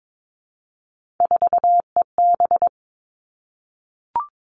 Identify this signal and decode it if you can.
Morse code, heard as a keyed tone, about 22 wpm, 700 Hz. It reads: 4EB
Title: All Cut Numbers - 3-Digit 22wpm